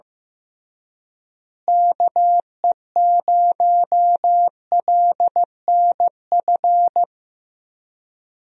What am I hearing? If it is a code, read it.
Morse code, heard as a keyed tone, about 15 wpm, 700 Hz. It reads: KE0LNF
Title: USA Call Signs - 6 Characters - Repeated 3x - 15wpm